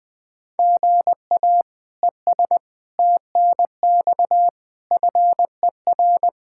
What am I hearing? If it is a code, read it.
Morse code, heard as a keyed tone, about 20 wpm, 700 Hz. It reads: GA ES TNX FER
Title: Practice QSO - Exchange 1 - Beginning - 20wpm